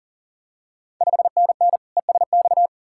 Morse code, heard as a keyed tone, about 38 wpm, 700 Hz. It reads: 5NN ESX
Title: Ontario QSO Party Exchange 40wpm